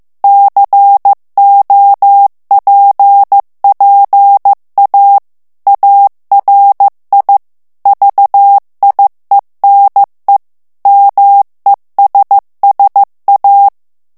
Morse code, heard as a keyed tone, about 15 wpm, 800 Hz. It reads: COPPA ARI VIENE MESSA
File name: morse mp3 / ITALIEN texte clair 2 vitesse 14 mots minute